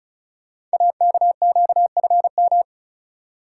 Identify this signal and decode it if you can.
Morse code, heard as a keyed tone, about 35 wpm, 690 Hz. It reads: AKQFM
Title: Code Groups - 5 Letter:Numbers - 35wpm